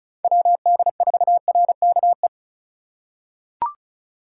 Morse code, heard as a keyed tone, about 36 wpm, 700 Hz. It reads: WD4RKE
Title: USA Call Signs - 3x Character Spacing - 35wpm